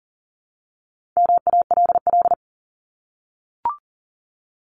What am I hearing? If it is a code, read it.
Morse code, heard as a keyed tone, about 40 wpm, 700 Hz. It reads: MALL
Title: Top 1501-1750 Words 40wpm